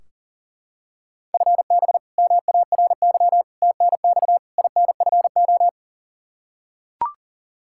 Morse code, heard as a keyed tone, about 40 wpm, 700 Hz. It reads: FB MARY TNX INFO
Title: Practice QSO - Exchange 3 - More Info - 40wpm